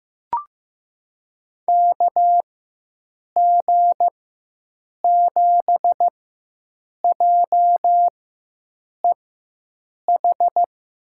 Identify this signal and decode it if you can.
Morse code, heard as a keyed tone, about 15 wpm, spaced as for 7 wpm, 700 Hz. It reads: KG7JEH
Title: USA Call Signs - 6 Characters - 4x Character Spacing - 15wpm